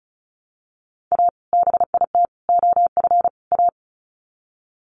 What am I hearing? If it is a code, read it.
Morse code, heard as a keyed tone, about 34 wpm, 700 Hz. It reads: A BIT OF A
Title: Sets of 4 Words 35wpm